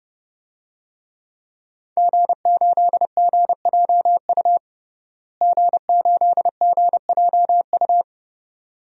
Morse code, heard as a keyed tone, about 30 wpm, 700 Hz. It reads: G8GJU G8GJU
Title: DXCC-100 Call Signs - 5 Characters - Repeated 2x - 30wpm